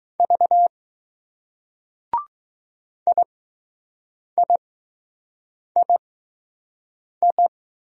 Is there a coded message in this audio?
Yes: Morse code from 0.2 to 7.5 s, heard as a keyed tone, about 20 wpm, spaced as for 6 wpm, 700 Hz. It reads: V IIII